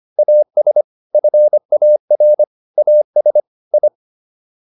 Morse code, heard as a keyed tone, about 25 wpm, 600 Hz. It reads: AS FAR AS I